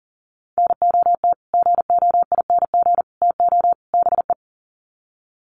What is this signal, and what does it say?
Morse code, heard as a keyed tone, about 40 wpm, 700 Hz. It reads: NOT GOING TO BE
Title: Sets of 4 Words 40wpm